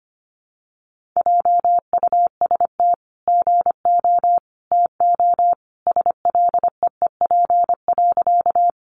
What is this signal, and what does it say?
Morse code, heard as a keyed tone, about 25 wpm, 700 Hz. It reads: JUST GO TO SLEEP.
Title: Sentences from Top 500 Words 25wpm